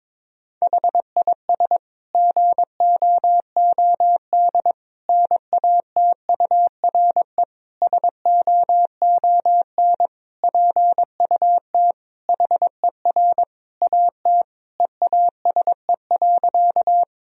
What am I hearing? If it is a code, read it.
Morse code, heard as a keyed tone, about 22 wpm, 700 Hz. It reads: HIS GOOD NATURE SOON PUT HER AT EASE.
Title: Sentences from Top 1000 Words - Review Part 1 - 22wpm